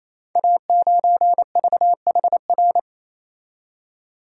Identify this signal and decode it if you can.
Morse code, heard as a keyed tone, about 27 wpm, 700 Hz. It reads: A9VHR
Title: Code Groups - 5 Letter:Numbers - 28wpm